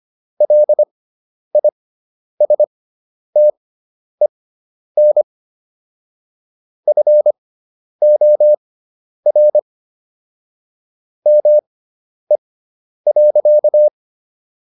Morse code, heard as a keyed tone, about 25 wpm, spaced as for 10 wpm, 600 Hz. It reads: LISTEN FOR ME.